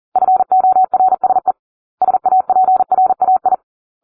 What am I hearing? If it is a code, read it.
Morse code, heard as a keyed tone, about 40 wpm, 750 Hz. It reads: FORSE SAPRAI